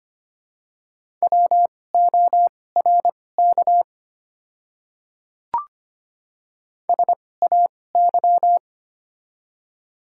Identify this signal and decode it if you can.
Morse code, heard as a keyed tone, about 25 wpm, spaced as for 18 wpm, 700 Hz. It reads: WORK SAY
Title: 2x Character Spacing - Top 300 Words - 25wpm